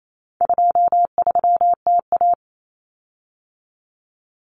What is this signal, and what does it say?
Morse code, heard as a keyed tone, about 28 wpm, 700 Hz. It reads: 23TA
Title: Common Cut Numbers - 4-Digit 28wpm